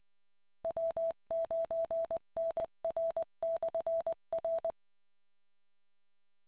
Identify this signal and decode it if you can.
Morse code, heard as a keyed tone, about 20 wpm, 660 Hz. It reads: W9NR/R